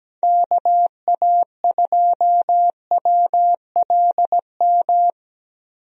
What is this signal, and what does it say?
Morse code, heard as a keyed tone, about 17 wpm, 700 Hz. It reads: KA2WLM
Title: USA Call Signs - 6 Characters - Repeated 3x - 17wpm